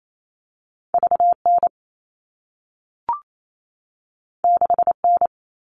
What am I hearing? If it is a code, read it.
Morse code, heard as a keyed tone, about 28 wpm, 700 Hz. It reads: VN 6N
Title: All Cut Numbers - 2-Digit 28wpm